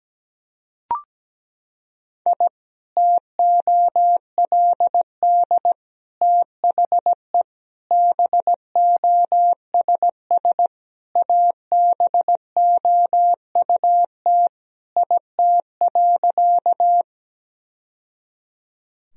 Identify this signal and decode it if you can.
Morse code, heard as a keyed tone, about 17 wpm, 700 Hz. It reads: I TOLD THE BOSS ABOUT IT.
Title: Sentences from Top 1000 Words - Review Part 4 - 17wpm